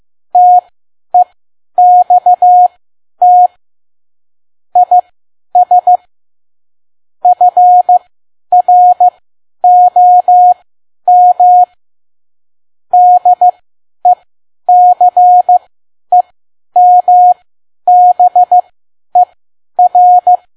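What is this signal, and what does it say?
Morse code, heard as a keyed tone, about 15 wpm, spaced as for 10 wpm, 720 Hz. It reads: TEXT IS FROM DECEMBER
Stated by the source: Morse code practice at 10 WPM.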